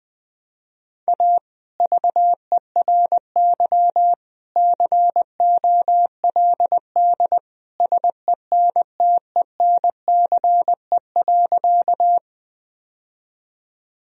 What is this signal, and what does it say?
Morse code, heard as a keyed tone, about 20 wpm, 700 Hz. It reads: A VERY COLD SENTENCE.
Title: Sentences from Top 1000 Words - Review Part 1 - 20wpm